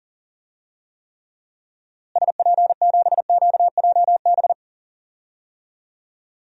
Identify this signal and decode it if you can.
Morse code, heard as a keyed tone, about 40 wpm, 700 Hz. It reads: SP7QJB
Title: DXCC-100 Call Signs - 6 Characters - Repeated 3x - 40wpm